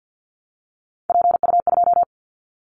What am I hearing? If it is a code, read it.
Morse code, heard as a keyed tone, about 50 wpm, 690 Hz. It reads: PU2
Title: Callsign Prefixes for Top 100 DXCC - 50wpm